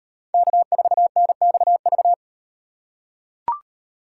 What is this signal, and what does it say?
Morse code, heard as a keyed tone, about 37 wpm, 700 Hz. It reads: K4NXV